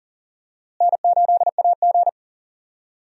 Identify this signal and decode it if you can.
Morse code, heard as a keyed tone, about 40 wpm, 700 Hz. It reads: N8AG
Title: USA Call Signs - 4 Characters - Repeated 2x - 40wpm